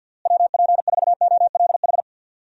Morse code, heard as a keyed tone, about 52 wpm, 700 Hz. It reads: WP4OLH
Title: USA Call Signs - 6 Characters - Repeated 3x - 50wpm